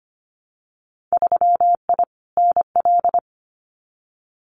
Morse code, heard as a keyed tone, about 25 wpm, 700 Hz. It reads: 3I NL
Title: Winter Field Day Exchanges 25wpm